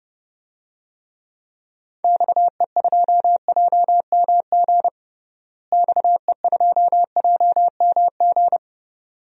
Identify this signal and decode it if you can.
Morse code, heard as a keyed tone, about 32 wpm, 700 Hz. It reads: XE2JMG XE2JMG
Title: DXCC-100 Call Signs - 6 Characters - Repeated 2x - 30wpm